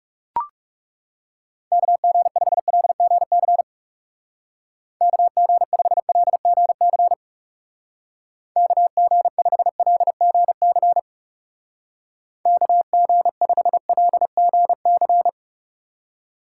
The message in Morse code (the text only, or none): KG5LGC KG5LGC KG5LGC KG5LGC